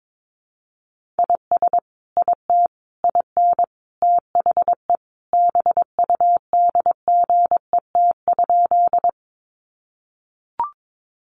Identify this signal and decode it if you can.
Morse code, heard as a keyed tone, about 22 wpm, 700 Hz. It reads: IS IT IN THE BUDGET?
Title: Sentences from Top 1200 Words 22wpm